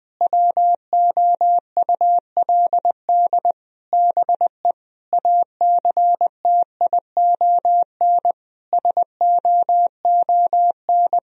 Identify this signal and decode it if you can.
Morse code, heard as a keyed tone, about 20 wpm, 700 Hz. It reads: WOULD BE ACTION SOON
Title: Sentences from Top 1000 Words - Review Part 4 - 20wpm